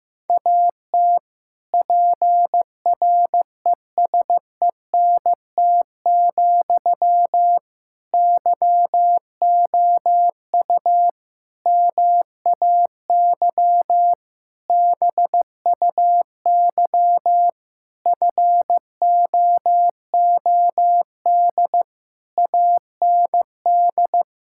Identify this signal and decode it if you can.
Morse code, heard as a keyed tone, about 16 wpm, 700 Hz. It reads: AT PRESENT, YOU MAY BUY FOOD AND
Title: Sentences from Top 1000 Words - Review Part 5 - 15wpm